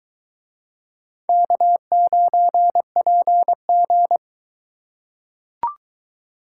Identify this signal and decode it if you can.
Morse code, heard as a keyed tone, about 21 wpm, 700 Hz. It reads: K9PG